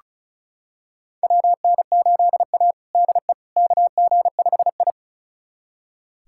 Morse code, heard as a keyed tone, about 36 wpm, 700 Hz. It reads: WN8A DE KG5I
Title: USA Callsign DE Callsign - 4 Characters - 35wpm